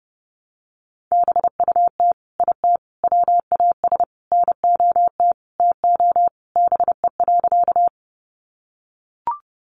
Morse code, heard as a keyed tone, about 30 wpm, 700 Hz. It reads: BUT IT WAS NOT TO BE.
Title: Sentences from Top 600 Words 30wpm